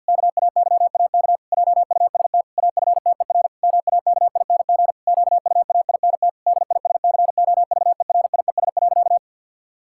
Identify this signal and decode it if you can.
Morse code, heard as a keyed tone, about 52 wpm, 700 Hz. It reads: KAYAK JUST AFTER MAKING QUAINT DISCOVERIES.
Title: Pangram Challenge v2 - 50wpm